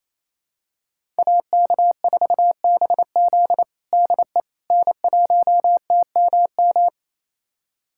Morse code, heard as a keyed tone, about 28 wpm, 700 Hz. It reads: AK4BZ DE N1TMM
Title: USA Callsign DE Callsign - 5 Characters - 28wpm